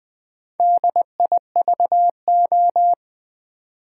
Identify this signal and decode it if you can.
Morse code, heard as a keyed tone, about 20 wpm, 700 Hz. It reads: DIVO
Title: Build A Word - 2 Letters At A Time - 8-Letter Words - 20wpm